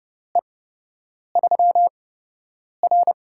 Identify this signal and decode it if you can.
Morse code, heard as a keyed tone, about 31 wpm, spaced as for 8 wpm, 700 Hz. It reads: E3R
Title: USA Call Signs - 8x Character Spacing - 30wpm